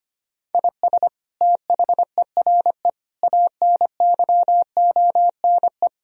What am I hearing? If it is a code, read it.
Morse code, heard as a keyed tone, about 25 wpm, 700 Hz. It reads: IS THERE ANYONE
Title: Sentences from Top 1000 Words - Review Part 3 - 25wpm